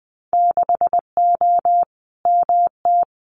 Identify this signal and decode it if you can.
Morse code, heard as a keyed tone, about 20 wpm, 700 Hz. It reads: 6O MT
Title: Winter Field Day Exchanges Encore 20wpm